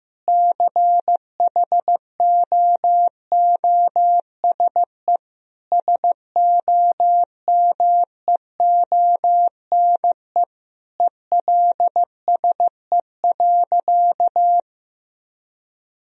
Morse code, heard as a keyed tone, about 16 wpm, 700 Hz. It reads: CHOOSE SOMEONE ELSE.
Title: Sentences from Top 1000 Words - Review Part 4 - 15wpm